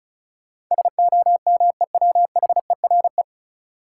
Morse code, heard as a keyed tone, about 36 wpm, 700 Hz. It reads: SOMEWHERE
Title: Sentences from Top 1400 Words - New Word Drill Alt - 35wpm